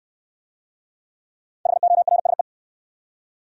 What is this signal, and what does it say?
Morse code, heard as a keyed tone, about 68 wpm, 700 Hz. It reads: HORSE